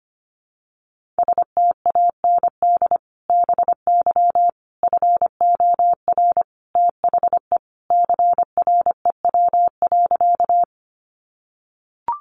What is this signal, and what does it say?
Morse code, heard as a keyed tone, about 25 wpm, 700 Hz. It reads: STAND BY FOR THE CREW.
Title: Sentences from Top 1500 Words 25wpm